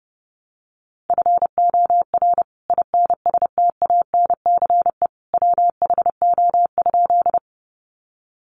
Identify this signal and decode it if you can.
Morse code, heard as a keyed tone, about 30 wpm, 700 Hz. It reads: FOR INSTANCE WHO?
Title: Sentences from Top 800 Words 30wpm